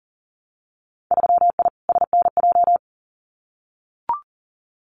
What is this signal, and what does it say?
Morse code, heard as a keyed tone, about 40 wpm, 700 Hz. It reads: 3I SNJ
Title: Winter Field Day Exchanges Encore 40wpm